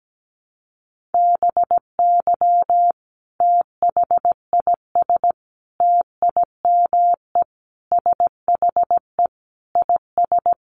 Morse code, heard as a keyed tone, about 17 wpm, 700 Hz. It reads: BY THIS TIME SHE IS
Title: Sentences from Top 700 Words 17wpm